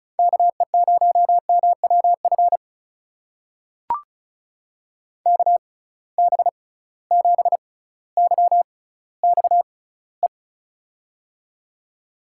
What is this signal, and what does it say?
Morse code, heard as a keyed tone, about 36 wpm, spaced as for 18 wpm, 700 Hz. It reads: KE0MWF KB7YXE
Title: USA Call Signs - 6x Character Spacing - 35wpm